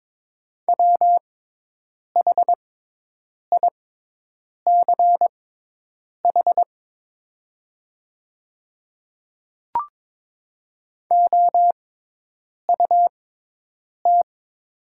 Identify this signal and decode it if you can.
Morse code, heard as a keyed tone, about 22 wpm, spaced as for 8 wpm, 700 Hz. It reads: WHICH OUT
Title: 6x Character Spacing - Top 300 Words - 22wpm